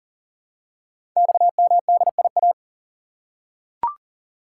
Morse code, heard as a keyed tone, about 41 wpm, 700 Hz. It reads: XMDIA
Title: Code Groups - 5 Letters - 40wpm